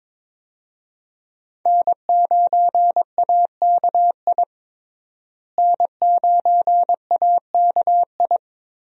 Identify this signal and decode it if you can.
Morse code, heard as a keyed tone, about 23 wpm, 700 Hz. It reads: N9AKI N9AKI
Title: USA Call Signs - 5 Characters - Repeated 3x - 22wpm